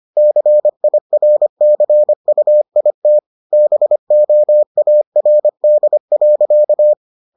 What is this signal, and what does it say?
Morse code, heard as a keyed tone, about 25 wpm, 600 Hz. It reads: CIRCUIT BOARD.